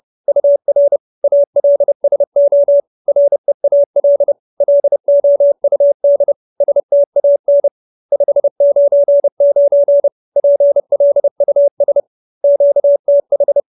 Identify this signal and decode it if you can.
Morse code, heard as a keyed tone, about 30 wpm, 570 Hz. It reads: UR ALSO REAL LOUD STAN 599 PLUS QTH
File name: QSO311 30